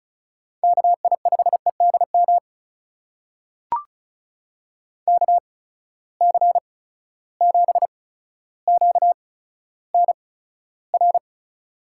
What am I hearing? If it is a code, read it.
Morse code, heard as a keyed tone, about 36 wpm, spaced as for 15 wpm, 700 Hz. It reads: KI5EDM KC7QNR
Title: USA Call Signs - 6 Characters - 8x Character Spacing - 35wpm